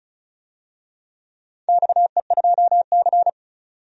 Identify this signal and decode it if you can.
Morse code, heard as a keyed tone, about 36 wpm, 700 Hz. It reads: XE2C
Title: DXCC-100 Call Signs - 4 Characters - 35wpm